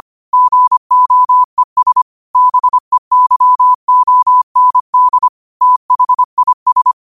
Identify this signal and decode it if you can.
Morse code, heard as a keyed tone, about 25 wpm, 1000 Hz. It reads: GOES BEYOND THIS
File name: qotdfiles / 25wpm